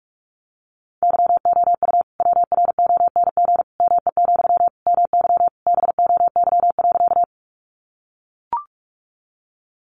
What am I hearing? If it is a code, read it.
Morse code, heard as a keyed tone, about 44 wpm, 700 Hz. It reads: YOU WRONG ME, MY BOY.
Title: Sentences from Top 900 Words 45wpm